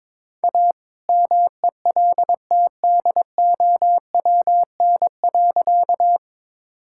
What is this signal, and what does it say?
Morse code, heard as a keyed tone, about 22 wpm, 700 Hz. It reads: A MELTDOWN.
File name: All-Age Appropriate Jokes - 22wpm